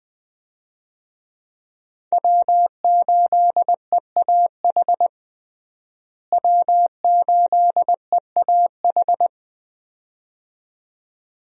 Morse code, heard as a keyed tone, about 20 wpm, 700 Hz. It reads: W8EAH W8EAH
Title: USA Call Signs - 5 Characters - Repeated 2x - 20wpm